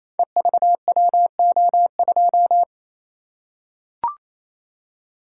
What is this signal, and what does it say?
Morse code, heard as a keyed tone, about 27 wpm, 700 Hz. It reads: EVWO2
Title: Code Groups - 5 Letter:Numbers - 28wpm